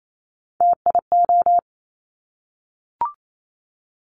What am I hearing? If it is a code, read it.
Morse code, heard as a keyed tone, about 28 wpm, 700 Hz. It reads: TIO
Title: Top Three Letter Combinations in Words 28wpm